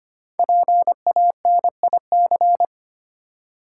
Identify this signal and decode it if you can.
Morse code, heard as a keyed tone, about 25 wpm, 700 Hz. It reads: PANIC
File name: 5 Letter Words - First 3 Letters Given - 25wpm